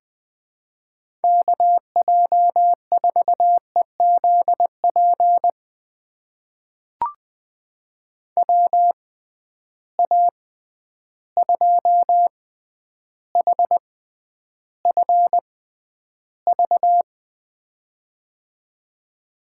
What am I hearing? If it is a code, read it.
Morse code, heard as a keyed tone, about 20 wpm, spaced as for 10 wpm, 700 Hz. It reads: KJ4EZP WA2HFV
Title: USA Call Signs - 6 Characters - 6x Character Spacing - 20wpm